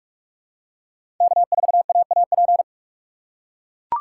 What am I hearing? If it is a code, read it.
Morse code, heard as a keyed tone, about 43 wpm, 700 Hz. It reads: K4AAP
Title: USA Call Signs - 5 Characters - 4x Character Spacing - 45wpm